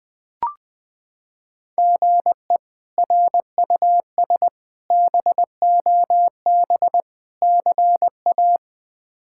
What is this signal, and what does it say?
Morse code, heard as a keyed tone, about 21 wpm, 700 Hz. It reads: GE RUS BOB CA
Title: K1USN SST QSO Practice - 20wpm